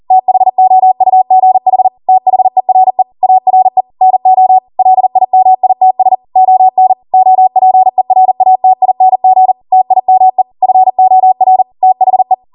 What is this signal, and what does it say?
Morse code, heard as a keyed tone, about 40 wpm, 750 Hz. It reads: THOUGH THERE ARE NO LIMITS ON OPERATING TIME FOR THE